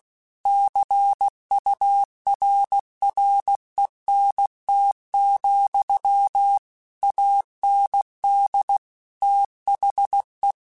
Morse code, heard as a keyed tone, about 16 wpm, 800 Hz. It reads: CURRENT, AND THE
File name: Technical Subjects / Technical Subjects-mp3s / XL @15